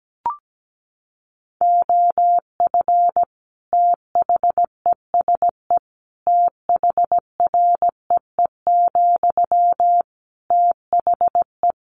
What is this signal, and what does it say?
Morse code, heard as a keyed tone, about 17 wpm, 700 Hz. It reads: OF THESE THREE, THE
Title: Sentences from Top 700 Words 17wpm